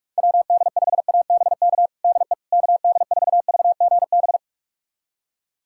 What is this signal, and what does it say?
Morse code, heard as a keyed tone, about 41 wpm, 700 Hz. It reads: WD5ABK DE KD4VGB
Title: USA Callsign DE Callsign 45wpm